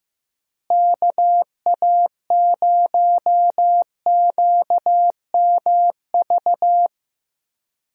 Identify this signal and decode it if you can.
Morse code, heard as a keyed tone, about 15 wpm, 700 Hz. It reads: KA0QMV
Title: USA Call Signs - 6 Characters - Repeated 3x - 15wpm